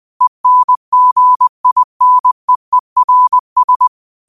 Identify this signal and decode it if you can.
Morse code, heard as a keyed tone, about 20 wpm, 1000 Hz. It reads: ENGINEERS
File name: qotdfiles / 20wpm